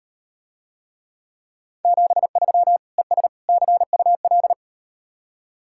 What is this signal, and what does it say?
Morse code, heard as a keyed tone, about 38 wpm, 700 Hz. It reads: 73 ES CUL